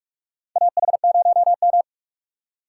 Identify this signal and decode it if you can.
Morse code, heard as a keyed tone, about 48 wpm, 700 Hz. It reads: AH0M
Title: USA Call Signs - 4 Characters - 8x Character Spacing - 45wpm